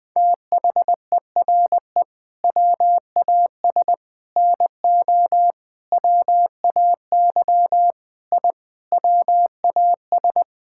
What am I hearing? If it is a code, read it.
Morse code, heard as a keyed tone, about 21 wpm, 700 Hz. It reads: THERE WAS NO WAY I WAS
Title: Sets of 8 Words - 20wpm